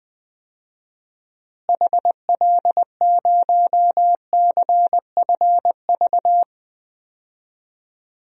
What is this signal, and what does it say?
Morse code, heard as a keyed tone, about 20 wpm, 700 Hz. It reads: HL0CFV
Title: DXCC-100 Call Signs - 6 Characters - Repeated 2x - 20wpm